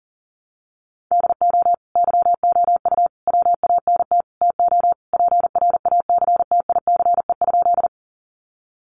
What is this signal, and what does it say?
Morse code, heard as a keyed tone, about 40 wpm, 700 Hz. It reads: DO YOU WANT TO PRACTICE?
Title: Sentences from Top 700 Words 40wpm